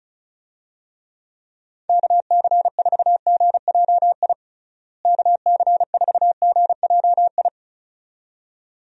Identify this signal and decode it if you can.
Morse code, heard as a keyed tone, about 34 wpm, 700 Hz. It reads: KC4GJI KC4GJI
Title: USA Call Signs - 6 Characters - Repeated 2x - 35wpm